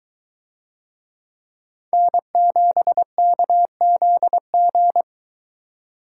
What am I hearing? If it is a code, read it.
Morse code, heard as a keyed tone, about 22 wpm, 700 Hz. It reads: N7KZG